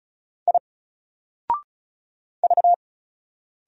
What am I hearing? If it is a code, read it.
Morse code, heard as a keyed tone, about 36 wpm, 700 Hz. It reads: I V
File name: Single-Letter-Number-35wpm